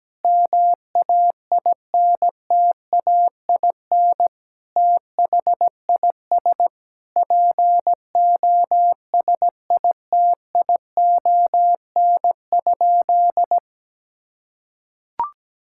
Morse code, heard as a keyed tone, about 17 wpm, 700 Hz. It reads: MAINTAIN THIS POSITION?
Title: Sentences from Top 1000 Words - Review Part 5 - 17wpm